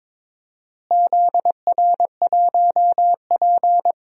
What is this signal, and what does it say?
Morse code, heard as a keyed tone, about 21 wpm, 700 Hz. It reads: ZR1P
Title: DXCC-100 Call Signs - 4 Characters - Repeated 3x - 22wpm